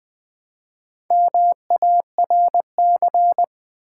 Morse code, heard as a keyed tone, about 20 wpm, 700 Hz. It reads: MARC
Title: Build A Word - 1 Letter At A Time - 3 to 5-Letter Words - 20wpm